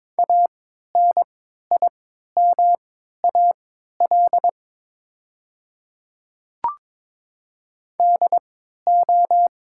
Morse code, heard as a keyed tone, about 23 wpm, spaced as for 13 wpm, 700 Hz. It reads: ANIMAL DO
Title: 3x Character Spacing - Top 200 Words - 22wpm